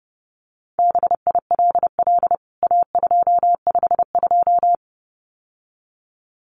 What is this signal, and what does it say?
Morse code, heard as a keyed tone, about 30 wpm, 700 Hz. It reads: BILL A252
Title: CWT Exchange 30wpm